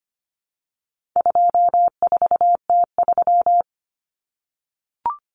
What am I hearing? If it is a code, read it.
Morse code, heard as a keyed tone, about 25 wpm, 700 Hz. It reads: 24T3
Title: Common Cut Numbers - 4-Digit 25wpm